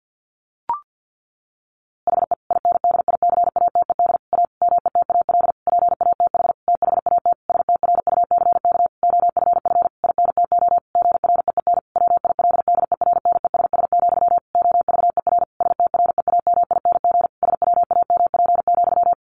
Morse code, heard as a keyed tone, about 50 wpm, 700 Hz. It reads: HE INDICATED A METAL PATH THAT STRUCK OFF INTO GREEN WILDERNESS, OVER STREAMING SWAMP,
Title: A Sound Of Thunder 50wpm